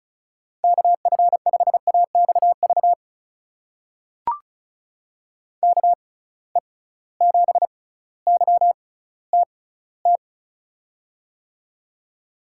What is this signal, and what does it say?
Morse code, heard as a keyed tone, about 36 wpm, spaced as for 18 wpm, 700 Hz. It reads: KF5AXV KE7YTT
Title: USA Call Signs - 6 Characters - 6x Character Spacing - 35wpm